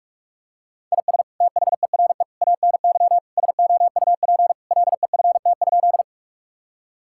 Morse code, heard as a keyed tone, about 41 wpm, 700 Hz. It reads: IS THERE ANY SOUP LEFT?
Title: Sentences from Top 1000 Words - Review Part 6 - 45wpm